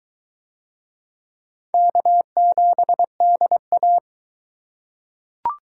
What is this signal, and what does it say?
Morse code, heard as a keyed tone, about 22 wpm, 700 Hz. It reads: K7DA